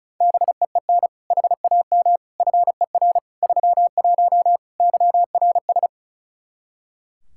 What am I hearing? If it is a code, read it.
Morse code, heard as a keyed tone, about 36 wpm, 700 Hz. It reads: BEEN HAM FER 31 YRS
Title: K1USN SST QSO Practice 35wpm